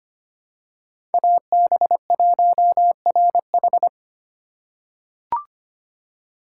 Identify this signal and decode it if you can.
Morse code, heard as a keyed tone, about 25 wpm, 700 Hz. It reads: AB1RH
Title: USA Call Signs - 5 Characters - 3x Character Spacing - 25wpm